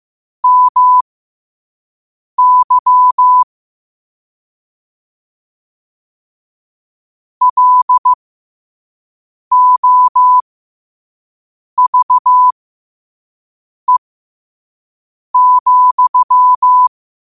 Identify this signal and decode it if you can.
Morse code, heard as a keyed tone, about 15 wpm, spaced as for 5 wpm, 1000 Hz. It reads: MY LOVE,
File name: qotdfiles / 5wpm